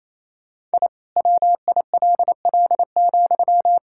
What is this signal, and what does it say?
Morse code, heard as a keyed tone, about 28 wpm, 700 Hz. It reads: I WILL,
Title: Sentences from Top 1000 Words - Review Part 5 - 28wpm